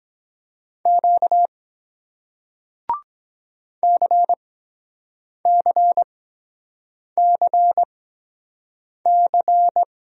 Morse code, heard as a keyed tone, about 23 wpm, spaced as for 7 wpm, 700 Hz. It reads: Q CCCC